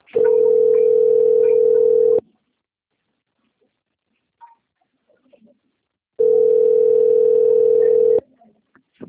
Ringback tone (North America), 440 + 480 Hz.